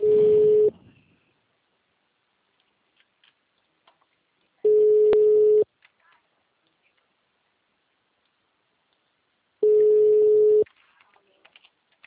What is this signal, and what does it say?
Ringback tone (Europe, 425 Hz), with speech.